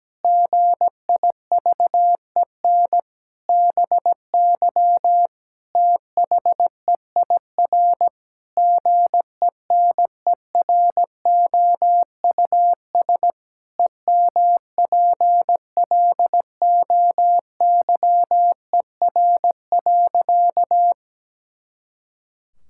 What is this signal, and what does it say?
Morse code, heard as a keyed tone, about 17 wpm, 700 Hz. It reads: GIVEN BY THEIR GENEROUS EMPLOYER.
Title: Pangram Challenge v2 - 17wpm